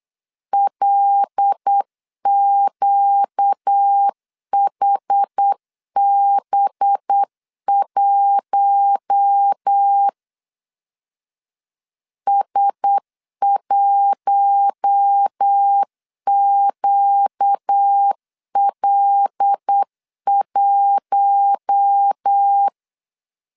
Morse code, heard as a keyed tone, about 8 wpm, 800 Hz. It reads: LQHB1 S1QL1
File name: CW / DET / 1 6 1500C S B 8wpm 5142